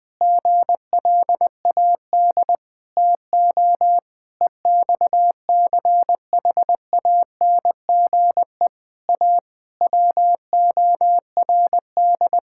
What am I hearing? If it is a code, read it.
Morse code, heard as a keyed tone, about 20 wpm, 700 Hz. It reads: GLAD TO EXCHANGE A WORD
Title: Sentences from Top 1000 Words - Review Part 6 - 20wpm